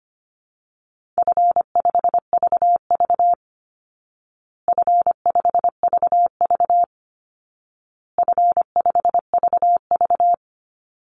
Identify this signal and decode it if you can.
Morse code, heard as a keyed tone, about 25 wpm, 700 Hz. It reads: F5VV F5VV F5VV
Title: Easy Call Signs Repeated 3x - 25wpm